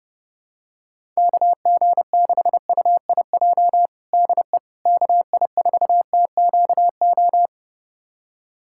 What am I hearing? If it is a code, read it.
Morse code, heard as a keyed tone, about 30 wpm, 700 Hz. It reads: KG6UIJ DE KI4TQO
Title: USA Callsign DE Callsign - 6 Characters - 30wpm